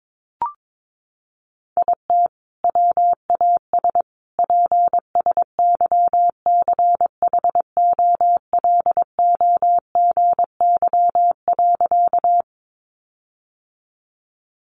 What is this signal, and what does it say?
Morse code, heard as a keyed tone, about 22 wpm, 700 Hz. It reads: IT WAS PSYCHOLOGY.
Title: Sentences from Top 1400 Words 22wpm